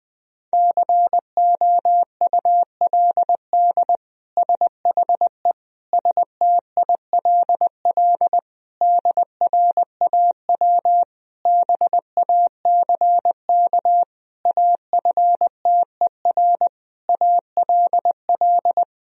Morse code, heard as a keyed tone, about 21 wpm, 700 Hz. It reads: COULD SHE STILL DRAW BACK AFTER ALL
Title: Sentences from Top 1000 Words - Review Part 1 - 20wpm